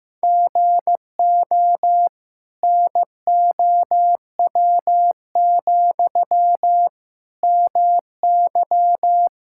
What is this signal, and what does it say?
Morse code, heard as a keyed tone, about 16 wpm, 700 Hz. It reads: GO NOW, MY
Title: Sentences from Top 1000 Words - Review Part 5 - 15wpm